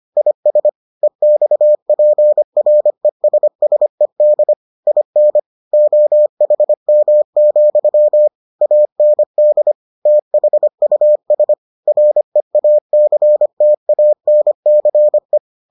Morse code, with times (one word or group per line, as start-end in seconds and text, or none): 0.2-0.7 IS
1.0-4.5 EXPRESSED
4.9-5.4 IN
5.7-8.3 OHM,
8.6-9.7 AND
10.1-11.5 THUS
11.9-15.4 REACTANCE